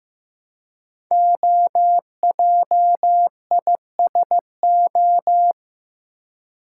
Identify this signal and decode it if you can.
Morse code, heard as a keyed tone, about 15 wpm, 700 Hz. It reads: OJISO
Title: Code Groups - 5 Letters - 15wpm